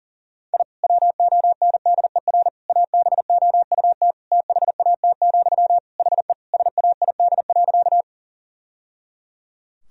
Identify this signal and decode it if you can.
Morse code, heard as a keyed tone, about 40 wpm, 700 Hz. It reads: I WONDER ABOUT THAT, HE SAID.
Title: Sentences from Top 1000 Words - Review Part 3 - 40wpm